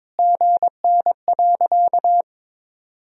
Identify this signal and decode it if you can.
Morse code, heard as a keyed tone, about 21 wpm, 700 Hz. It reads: GN.
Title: Sentences from Top 300 Words and Abbreviations 22wpm